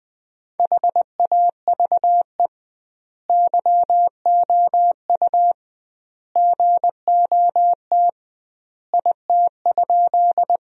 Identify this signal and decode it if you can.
Morse code, heard as a keyed tone, about 20 wpm, 700 Hz. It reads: HAVE YOU GOT IT?
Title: 2x Word-Spacing - Sentences from Top 300 Words - 20wpm